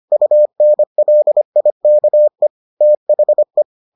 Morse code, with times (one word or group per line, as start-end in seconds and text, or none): 0.1-2.5 UNLIKE
2.8-3.6 THE